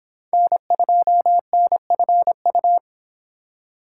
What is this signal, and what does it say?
Morse code, heard as a keyed tone, about 26 wpm, 700 Hz. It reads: N2NFU